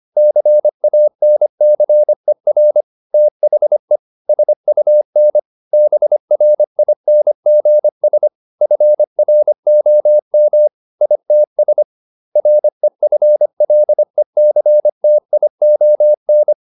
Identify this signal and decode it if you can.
Morse code, heard as a keyed tone, about 25 wpm, 600 Hz. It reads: CANCER THE SUN BRINGS FROM ITS REFLECTION
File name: the old man and the sea part 1 25wpm